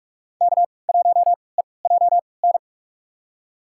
Morse code, heard as a keyed tone, about 46 wpm, spaced as for 25 wpm, 700 Hz. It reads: K1EJN
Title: USA Call Signs - 5 Characters - 3x Character Spacing - 45wpm